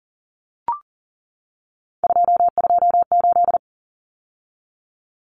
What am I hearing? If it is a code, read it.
Morse code, heard as a keyed tone, about 40 wpm, 700 Hz. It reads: 228
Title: Numbers - 3-Digit 40wpm